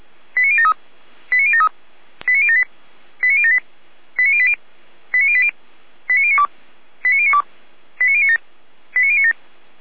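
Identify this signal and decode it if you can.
Five-tone selective call (ZVEI), heard as quick runs of five tones, tones of about 70 ms. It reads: 89073 89073 89078 89078 89080 89080 89092 89092 89097 89097
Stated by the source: ZVEI five-tone selective call 89067 received by SDR, Feuerwehr Orlamünde.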